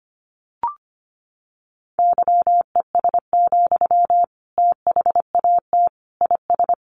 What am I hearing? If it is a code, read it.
Morse code, heard as a keyed tone, about 25 wpm, 700 Hz. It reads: YES, THAT IS
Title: Sentences from Top 500 Words 25wpm